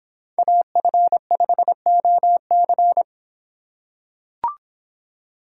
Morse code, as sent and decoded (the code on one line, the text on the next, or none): .- ..-. ..... --- -.-.
AF5OC